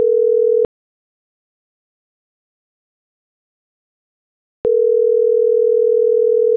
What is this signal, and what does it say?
Ringback tone (North America), 440 + 480 Hz.
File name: CENTRAL MABE / out miriam.soledad - .62270